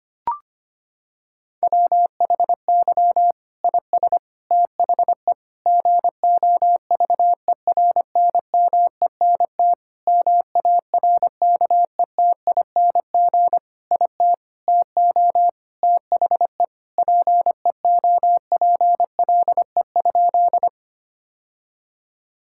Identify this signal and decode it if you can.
Morse code, heard as a keyed tone, about 25 wpm, 700 Hz. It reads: WHY IS THE GOVERNMENT MARKETING IT TO THE PEOPLE?
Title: Sentences from Top 1000 Words - Review Part 5 - 25wpm